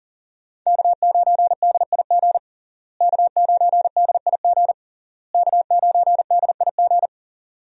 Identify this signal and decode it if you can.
Morse code, heard as a keyed tone, about 40 wpm, 700 Hz. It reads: K9DIG K9DIG K9DIG
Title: USA Call Signs - 5 Characters - Repeated 3x - 40wpm